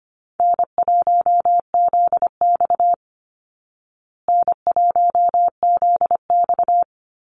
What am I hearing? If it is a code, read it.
Morse code, heard as a keyed tone, about 25 wpm, 700 Hz. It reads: N1ZX N1ZX
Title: Easy Call Signs Repeated 2x - 25wpm